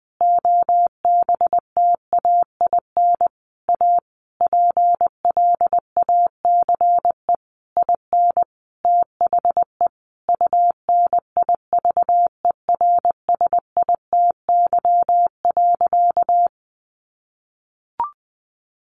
Morse code, heard as a keyed tone, about 20 wpm, 700 Hz. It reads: OBTAIN A PLACE IN THE UNIVERSITY.
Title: Sentences from Top 1300 Words 20wpm